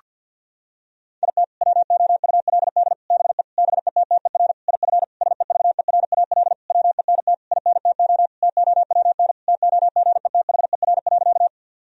Morse code, heard as a keyed tone, about 50 wpm, 700 Hz. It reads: IT WOULD BE BETTER IF SEVERAL WENT INTO TOWN TOGETHER.
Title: Sentences from Top 1000 Words - Review Part 2 - 50wpm